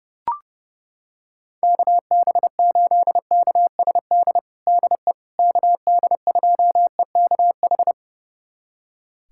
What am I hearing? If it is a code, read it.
Morse code, heard as a keyed tone, about 30 wpm, 700 Hz. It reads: KB8KSD DE KD2EKH
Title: USA Callsign DE Callsign 30wpm